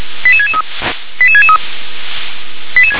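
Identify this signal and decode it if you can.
Five-tone selective call (ZVEI), heard as quick runs of five tones, tones of about 70 ms. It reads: 88663 88663
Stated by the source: ZVEI five-tone selective call 88696 received on an SDR.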